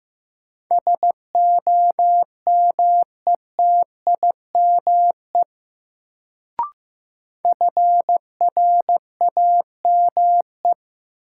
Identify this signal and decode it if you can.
Morse code, heard as a keyed tone, about 15 wpm, 700 Hz. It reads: SOMETIME FRAME
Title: Sentences from Top 1000 Words - Review Part 6 - New Word Drill - 15wpm